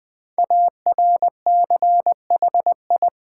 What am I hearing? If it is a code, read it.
Morse code, heard as a keyed tone, about 20 wpm, 700 Hz. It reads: ARCHI
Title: Build A Word - 5 Letters At A Time - 12-Letter Words - 20wpm